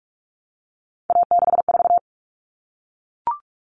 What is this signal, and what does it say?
Morse code, heard as a keyed tone, about 44 wpm, 700 Hz. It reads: A64
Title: All Cut Numbers - 3-Digit 45wpm